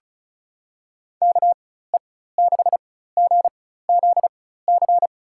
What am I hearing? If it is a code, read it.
Morse code, heard as a keyed tone, about 36 wpm, spaced as for 16 wpm, 700 Hz. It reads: KE6GZC
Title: USA Call Signs - 6 Characters - 4x Character Spacing - 35wpm